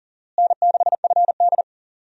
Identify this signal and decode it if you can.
Morse code, heard as a keyed tone, about 40 wpm, 700 Hz. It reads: N6FD
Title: USA Call Signs - 4 Characters - Repeated 2x - 40wpm